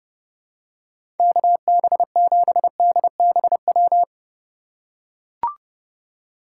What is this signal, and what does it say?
Morse code, heard as a keyed tone, about 30 wpm, 700 Hz. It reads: KB7DBW
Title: USA Call Signs - 6 Characters - 4x Character Spacing - 30wpm